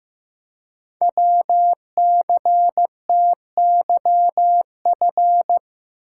Morse code, heard as a keyed tone, about 15 wpm, 700 Hz. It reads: WCTYF
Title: Code Groups - 5 Letter:Numbers - 15wpm